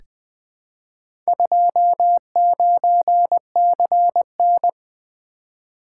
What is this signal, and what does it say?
Morse code, heard as a keyed tone, about 21 wpm, 700 Hz. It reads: 29CN
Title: Four Letter-Number Combinations - 20wpm